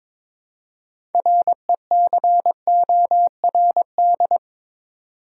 Morse code, heard as a keyed tone, about 26 wpm, 700 Hz. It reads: RECORD
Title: Speed-Racing Top 500 Words 22wpm